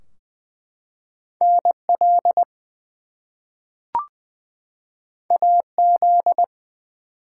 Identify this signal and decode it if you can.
Morse code, heard as a keyed tone, about 20 wpm, 700 Hz. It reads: NL AZ
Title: US States and Canadian Provinces 20wpm